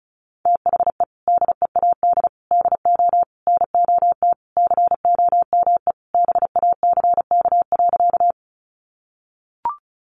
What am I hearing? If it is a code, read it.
Morse code, heard as a keyed tone, about 34 wpm, 700 Hz. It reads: THE DEAD DO NOT COME BACK.
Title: Sentences from Top 1500 Words 35wpm